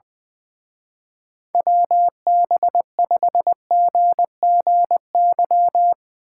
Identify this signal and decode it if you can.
Morse code, heard as a keyed tone, about 21 wpm, 700 Hz. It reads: WB5GGY
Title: Speed Racing - USA 2x3 Call Signs 20wpm